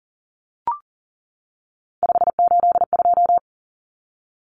Morse code, heard as a keyed tone, about 40 wpm, 700 Hz. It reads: 582
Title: Numbers - 3-Digit 40wpm